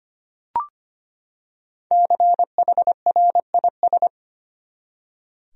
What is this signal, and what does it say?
Morse code, heard as a keyed tone, about 25 wpm, 700 Hz. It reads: CHRIS
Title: USA Common Names - Max 5 Letters - 25wpm